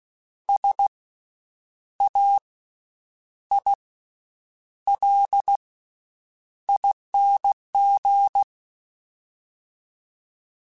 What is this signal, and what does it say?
Morse code, heard as a keyed tone, about 16 wpm, 800 Hz. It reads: S A I L ING
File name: Special INGs-mp3s / Special INGs-mp3s / Slow ing @15